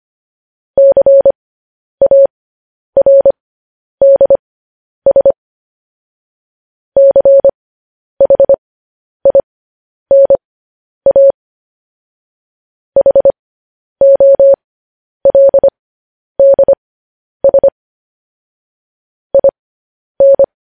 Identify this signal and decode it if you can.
Morse code, heard as a keyed tone, about 25 wpm, spaced as for 10 wpm, 550 Hz. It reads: CARDS CHINA HOLDS IN
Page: news headlines in morse code at 10 wpm